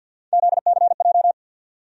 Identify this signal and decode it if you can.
Morse code, heard as a keyed tone, about 52 wpm, 700 Hz. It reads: ZCJ
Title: USA Call Sign Postfixes - 3 Letters 50wpm